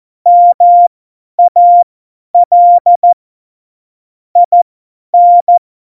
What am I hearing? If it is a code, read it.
Morse code, heard as a keyed tone, about 14 wpm, spaced as for 10 wpm, 700 Hz. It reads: MAL IN
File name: Morsekurs / Witze / Witz 15. August 17, Tempo 50 F70